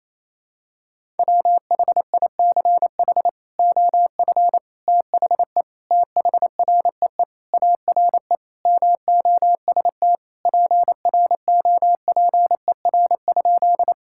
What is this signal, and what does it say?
Morse code, heard as a keyed tone, about 28 wpm, 700 Hz. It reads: WHICH OF THE THREE ARE MOST PROPER?
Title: Sentences from Top 1000 Words - Review Part 5 - 28wpm